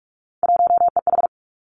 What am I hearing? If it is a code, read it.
Morse code, heard as a keyed tone, about 46 wpm, 700 Hz. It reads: 1EH
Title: Three Challenging Letter-Number Combinations 45wpm